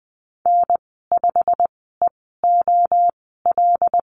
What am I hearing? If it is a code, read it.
Morse code, heard as a keyed tone, about 20 wpm, spaced as for 14 wpm, 700 Hz. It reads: N5EOL
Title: 088 Lesson 15g - Number 5 - Call Signs 20wpm